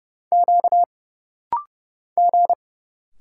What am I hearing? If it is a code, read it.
Morse code, heard as a keyed tone, about 30 wpm, 700 Hz. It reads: Q G
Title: Single Letters - Rapid-Fire 30wpm